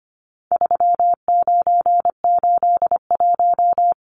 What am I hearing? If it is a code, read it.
Morse code, heard as a keyed tone, about 25 wpm, 700 Hz. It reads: 3981
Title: Numbers - 4-Digit 25wpm